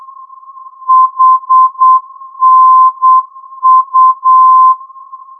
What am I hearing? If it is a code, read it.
Morse code, heard as a keyed tone, about 8 wpm, 1040 Hz. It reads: HNU
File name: ndb hanau hnu-432